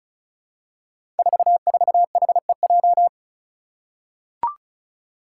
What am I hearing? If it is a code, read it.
Morse code, heard as a keyed tone, about 34 wpm, 700 Hz. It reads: 44HEJ
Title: Five Challenging Letter-Number Combinations 35wpm